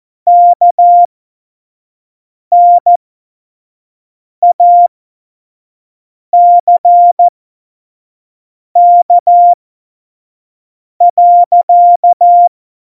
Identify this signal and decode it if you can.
Morse code, heard as a keyed tone, about 14 wpm, spaced as for 5 wpm, 700 Hz. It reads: KNACK.
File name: Morsekurs / Witze / Witz 13. November 2018, 25 F70